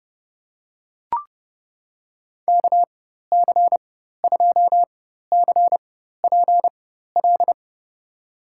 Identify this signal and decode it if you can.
Morse code, heard as a keyed tone, about 32 wpm, spaced as for 14 wpm, 700 Hz. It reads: KC2CPL
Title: USA Call Signs - 6 Characters - 4x Character Spacing - 30wpm